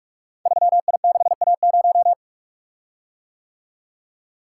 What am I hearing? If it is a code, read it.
Morse code, heard as a keyed tone, about 44 wpm, 690 Hz. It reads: 3I6A0
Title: Code Groups - 5 Letter:Numbers - 45wpm